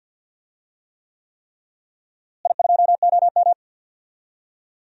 Morse code, heard as a keyed tone, about 47 wpm, 700 Hz. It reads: I2OM
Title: DXCC-100 Call Signs - 4 Characters - Repeated 3x - 50wpm